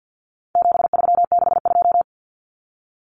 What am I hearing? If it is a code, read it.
Morse code, heard as a keyed tone, about 50 wpm, 700 Hz. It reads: 7362
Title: Numbers - 4-Digit 50wpm